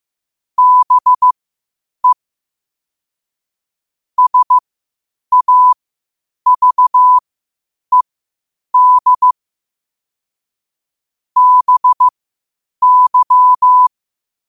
Morse code, heard as a keyed tone, about 15 wpm, spaced as for 8 wpm, 1000 Hz. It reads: BE SAVED BY